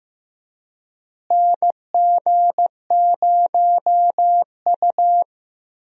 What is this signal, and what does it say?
Morse code, heard as a keyed tone, about 15 wpm, 700 Hz. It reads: NG0U
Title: USA Call Signs - 4 Characters - Repeated 3x - 15wpm